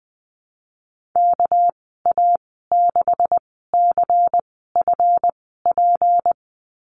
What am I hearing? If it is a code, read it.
Morse code, heard as a keyed tone, about 20 wpm, spaced as for 15 wpm, 700 Hz. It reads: KA6CFP
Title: 200 Lesson 32g - Letter k - Call Signs 20wpm